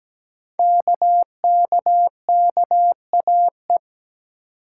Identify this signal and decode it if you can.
Morse code, heard as a keyed tone, about 17 wpm, 700 Hz. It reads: KKKAE
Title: Code Groups - 5 Letters - 17wpm